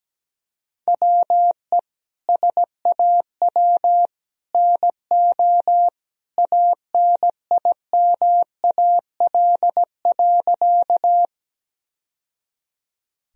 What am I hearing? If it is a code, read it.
Morse code, heard as a keyed tone, about 17 wpm, 700 Hz. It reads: WE SAW NO ANIMAL.
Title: Sentences from Top 1000 Words - Review Part 1 - 17wpm